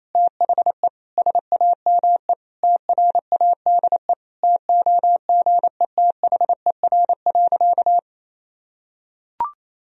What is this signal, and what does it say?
Morse code, heard as a keyed tone, about 28 wpm, 700 Hz. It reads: THE SAME TRADE TOGETHER.
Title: Sentences from Top 1000 Words - Review Part 4 - 28wpm